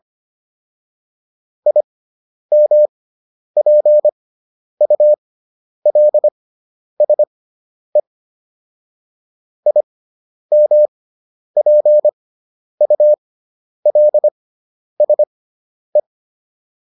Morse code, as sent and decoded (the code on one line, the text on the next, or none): .. -- .--. ..- .-.. ... / .. -- .--. ..- .-.. ... .
IMPULSE IMPULSE